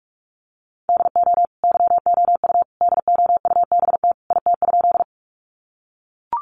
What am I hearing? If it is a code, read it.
Morse code, heard as a keyed tone, about 44 wpm, 700 Hz. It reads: DO YOU DOUBT IT?
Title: Sentences from Top 1500 Words 45wpm